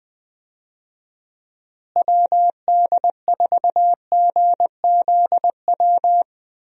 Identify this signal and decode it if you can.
Morse code, heard as a keyed tone, about 21 wpm, 700 Hz. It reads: WD4GZW
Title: USA Call Signs - 6 Characters - Repeated 3x - 20wpm